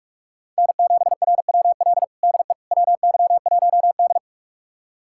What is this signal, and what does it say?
Morse code, heard as a keyed tone, about 41 wpm, 700 Hz. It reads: N7RWL DE WY1D
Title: USA Callsign DE Callsign 45wpm